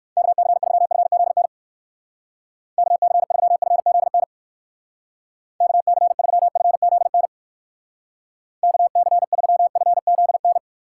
Morse code, heard as a keyed tone, about 57 wpm, 700 Hz. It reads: KC3FZN KC3FZN KC3FZN KC3FZN